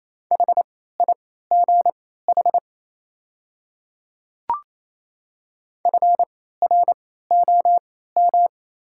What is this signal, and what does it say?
Morse code, heard as a keyed tone, about 27 wpm, spaced as for 16 wpm, 690 Hz. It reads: HIGH FROM
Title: 3x Character Spacing - Top 500 Words - 28wpm